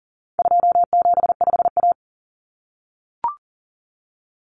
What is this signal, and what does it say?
Morse code, heard as a keyed tone, about 40 wpm, 700 Hz. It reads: 275A
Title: Common Cut Numbers - 4-Digit 40wpm